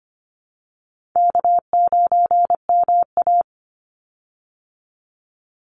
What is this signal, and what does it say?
Morse code, heard as a keyed tone, about 25 wpm, 700 Hz. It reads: K9MA
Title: Easy Call Signs Repeated 3x - 25wpm